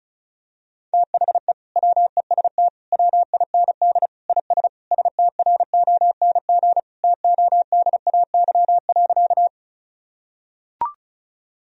Morse code, heard as a keyed tone, about 36 wpm, 700 Hz. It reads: THE WEST WIND IS STRONG TODAY.
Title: Sentences from Top 1000 Words - Review Part 2 - 35wpm